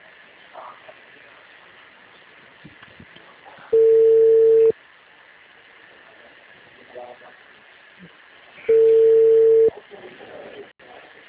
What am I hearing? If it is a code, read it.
Ringback tone (China, 450 Hz), with speech.
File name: out- -2017- - - .393